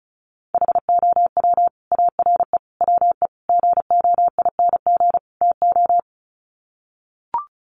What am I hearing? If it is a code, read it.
Morse code, heard as a keyed tone, about 34 wpm, 700 Hz. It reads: HOW ARE WE GOING TO
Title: Sets of 5 Words Encore 35wpm